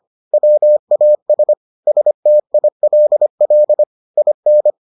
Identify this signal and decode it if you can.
Morse code, heard as a keyed tone, about 25 wpm, 600 Hz. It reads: WAS STILL IN